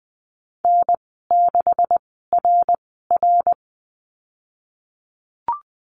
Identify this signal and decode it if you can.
Morse code, heard as a keyed tone, about 20 wpm, spaced as for 15 wpm, 700 Hz. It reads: N6RR
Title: 147 Lesson 24g - Letter y - Call Signs 20wpm